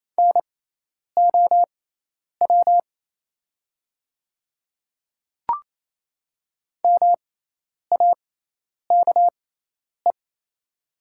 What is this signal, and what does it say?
Morse code, heard as a keyed tone, about 28 wpm, spaced as for 10 wpm, 690 Hz. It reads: NOW MAKE
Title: 6x Character Spacing - Top 200 Words - 28wpm